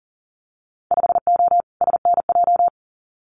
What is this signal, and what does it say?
Morse code, heard as a keyed tone, about 38 wpm, 700 Hz. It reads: 5O SNJ
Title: Winter Field Day Exchanges Encore 40wpm